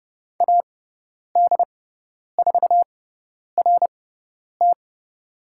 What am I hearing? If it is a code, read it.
Morse code, heard as a keyed tone, about 30 wpm, spaced as for 10 wpm, 700 Hz. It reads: AD4RT
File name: Lesson 16g - Letter c - Call Signs 30wpm